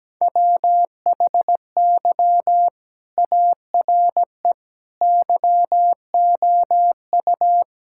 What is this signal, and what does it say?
Morse code, heard as a keyed tone, about 17 wpm, 700 Hz. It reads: WHY ARE YOU
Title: Sentences from Top 1000 Words - Review Part 5 - 17wpm